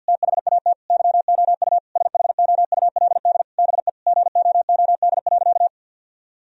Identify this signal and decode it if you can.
Morse code, heard as a keyed tone, about 52 wpm, 700 Hz. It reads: THAT YOU SHOULD BE GOOD.
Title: Sentences from Top 1000 Words - Review Part 1 - 50wpm